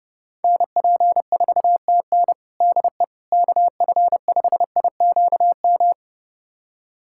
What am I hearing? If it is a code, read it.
Morse code, heard as a keyed tone, about 32 wpm, 700 Hz. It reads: NP4TN DE KF5IQM
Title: USA Callsign DE Callsign 30wpm